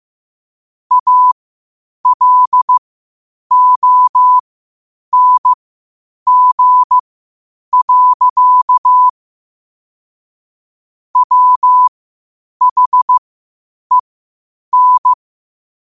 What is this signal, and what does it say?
Morse code, heard as a keyed tone, about 15 wpm, spaced as for 8 wpm, 1000 Hz. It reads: ALONG. WHEN
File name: qotdfiles / 8wpm